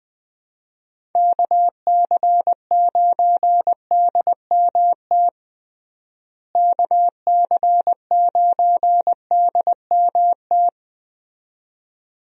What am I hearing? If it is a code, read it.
Morse code, heard as a keyed tone, about 21 wpm, 700 Hz. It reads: KC9DMT KC9DMT
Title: USA Call Signs - 6 Characters - Repeated 2x - 20wpm